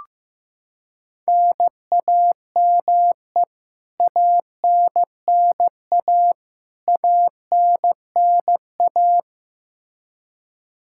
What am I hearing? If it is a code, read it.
Morse code, heard as a keyed tone, about 15 wpm, 700 Hz. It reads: NAME ANNA ANNA
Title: K1USN SST QSO Practice 15wpm